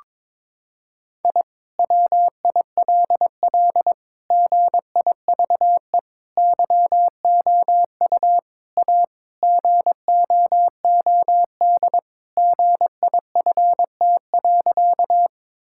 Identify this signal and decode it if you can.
Morse code, heard as a keyed tone, about 22 wpm, 700 Hz. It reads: I WILL GIVE YOU A GOOD GIFT.
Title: Sentences from Top 1000 Words - Review Part 6 - 22wpm